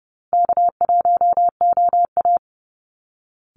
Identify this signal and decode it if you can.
Morse code, heard as a keyed tone, about 30 wpm, 700 Hz. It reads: K1OA
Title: Easy Call Signs Repeated 3x - 30wpm